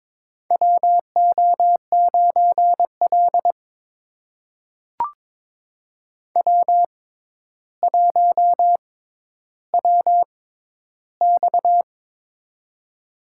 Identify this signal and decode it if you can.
Morse code, heard as a keyed tone, about 23 wpm, spaced as for 11 wpm, 700 Hz. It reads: WO9L W1WX
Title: USA Call Signs - 4 Characters - 6x Character Spacing - 22wpm